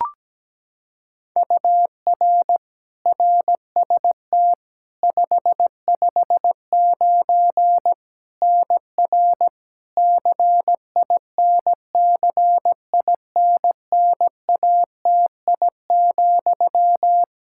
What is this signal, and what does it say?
Morse code, heard as a keyed tone, about 17 wpm, 700 Hz. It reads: UR RST 559 NR CINCINNATI,
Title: K1USN SST QSO Practice 17wpm